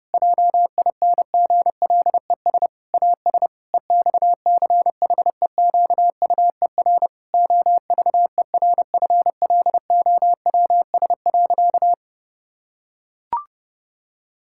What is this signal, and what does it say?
Morse code, heard as a keyed tone, about 30 wpm, 700 Hz. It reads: JINGLES AS EXCHEQUER OVERFLOWS.
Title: Pangram Challenge v2 - 30wpm